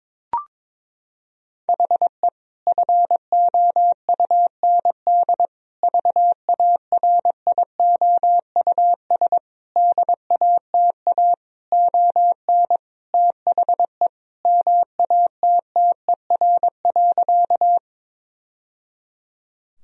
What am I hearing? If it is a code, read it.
Morse code, heard as a keyed tone, about 22 wpm, 700 Hz. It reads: HE FOUND VARIOUS DATA ON THE MATTER.
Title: Sentences from Top 1000 Words - Review Part 4 - 22wpm